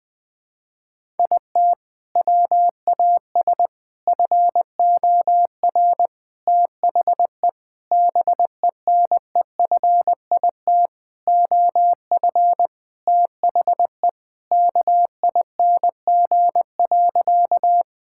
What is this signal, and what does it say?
Morse code, heard as a keyed tone, about 20 wpm, 700 Hz. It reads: IT WAS FOR THE BENEFIT OF THE KING.
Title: Sentences from Top 1000 Words - Review Part 6 - 20wpm